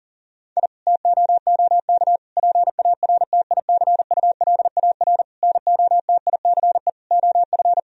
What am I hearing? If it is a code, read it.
Morse code, heard as a keyed tone, about 40 wpm, 700 Hz. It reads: I TOOK PARTICULAR NOTICE OF
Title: Sentences from Top 1000 Words - Review Part 4 - 40wpm